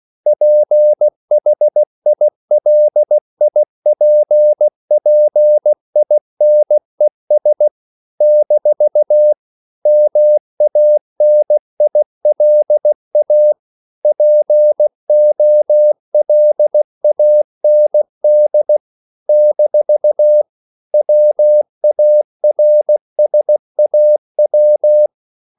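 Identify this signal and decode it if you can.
Morse code, heard as a keyed tone, about 16 wpm, 600 Hz. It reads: PHILIPPINES - MANILA POLAND - WARSAW
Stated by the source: CW practice of national capitals at 16 words per minute, from ON6ZQ.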